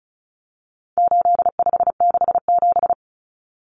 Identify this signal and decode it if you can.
Morse code, heard as a keyed tone, about 34 wpm, 700 Hz. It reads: 8567
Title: Numbers - 4-Digit 35wpm